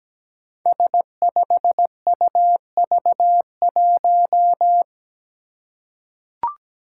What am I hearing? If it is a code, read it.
Morse code, heard as a keyed tone, about 17 wpm, 700 Hz. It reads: S5UV1
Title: Five Challenging Letter-Number Combinations 17wpm